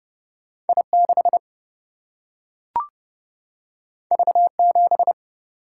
Morse code, heard as a keyed tone, about 30 wpm, 700 Hz. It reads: I6 V7
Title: Single-Letter Single-Number 30wpm